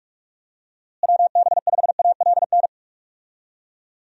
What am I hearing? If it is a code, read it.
Morse code, heard as a keyed tone, about 46 wpm, 700 Hz. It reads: WB5ALN
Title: USA Call Signs - 6 Characters - Repeated 2x - 45wpm